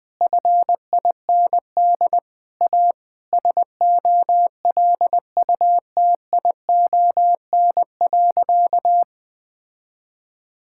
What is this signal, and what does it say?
Morse code, heard as a keyed tone, about 20 wpm, 700 Hz. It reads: FIND A SOLUTION.
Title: Sentences with 8-Word Phrases - 20wpm